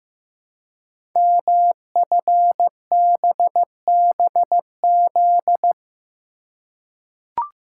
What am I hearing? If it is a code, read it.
Morse code, heard as a keyed tone, about 15 wpm, 700 Hz. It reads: MFBBZ
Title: Code Groups - 5 Letters - 15wpm